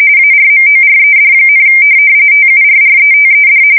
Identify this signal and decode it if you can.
RTTY, about 45.45 baud. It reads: ARMORED, TURRETED
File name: fsk170 / USS-New-Jersey